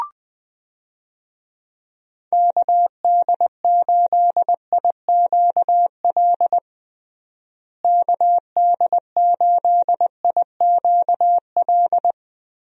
Morse code, heard as a keyed tone, about 21 wpm, 700 Hz. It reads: KD8IQL KD8IQL
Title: USA Call Signs - 6 Characters - Repeated 3x - 20wpm